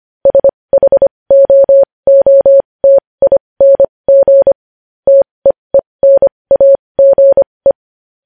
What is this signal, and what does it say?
Morse code, heard as a keyed tone, about 25 wpm, spaced as for 20 wpm, 550 Hz. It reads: SHOOTING TEENAGE
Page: News Headlines in Morse Code at 20 WPM